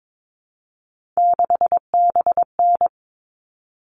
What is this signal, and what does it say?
Morse code, heard as a keyed tone, about 22 wpm, 700 Hz. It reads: 6BN
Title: All Cut Numbers - 3-Digit 22wpm